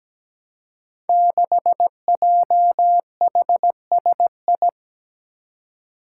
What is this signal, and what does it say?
Morse code, heard as a keyed tone, about 17 wpm, 700 Hz. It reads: 6JHSI
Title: Five Challenging Letter-Number Combinations 17wpm